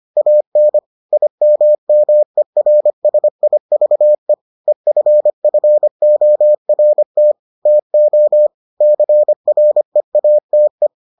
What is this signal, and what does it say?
Morse code, heard as a keyed tone, about 25 wpm, 600 Hz. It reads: AN IMMERSIVE EFFORT TO CREATE